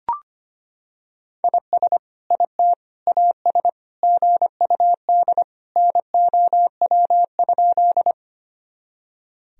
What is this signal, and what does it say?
Morse code, heard as a keyed tone, about 25 wpm, 700 Hz. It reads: IS IT AS GUD NOW?
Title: Sentences from Top 200 Words and Abbreviations 25wpm